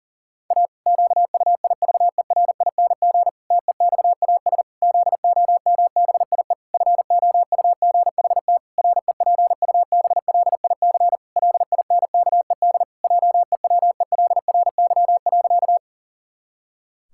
Morse code, heard as a keyed tone, about 40 wpm, 700 Hz. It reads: A QUIVERING TEXAS ZOMBIE FOUGHT REPUBLIC LINKED JEWELRY.
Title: Pangram Challenge v2 - 40wpm